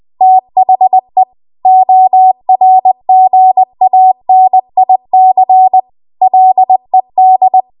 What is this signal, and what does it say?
Morse code, heard as a keyed tone, about 20 wpm, 750 Hz. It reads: THE ORGANIC LED